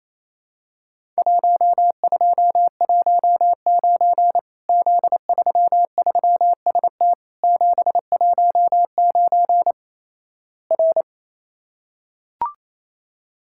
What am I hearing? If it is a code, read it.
Morse code, heard as a keyed tone, about 28 wpm, 700 Hz. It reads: 1219 Z33ST 719 R
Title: Worked All Europe DX Contest - QTC Exchange - 28wpm